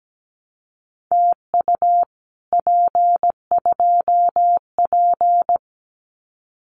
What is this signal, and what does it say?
Morse code, heard as a keyed tone, about 17 wpm, 700 Hz. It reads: TU P2P
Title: Common QSO Phrases Encore 17wpm